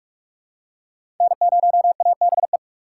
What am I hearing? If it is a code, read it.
Morse code, heard as a keyed tone, about 46 wpm, 690 Hz. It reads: N0ABE
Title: USA Call Signs - 4x Character Spacing - 45wpm